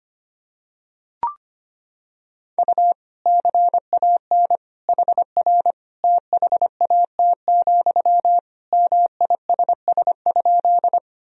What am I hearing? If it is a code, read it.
Morse code, heard as a keyed tone, about 25 wpm, 700 Hz. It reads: U CAN HR THAT, MISS?
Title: Sentences from Top 500 Words and Abbreviations 25wpm